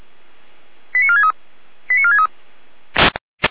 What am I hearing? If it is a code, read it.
Five-tone selective call (ZVEI), heard as quick runs of five tones, tones of about 70 ms. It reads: 89463 89463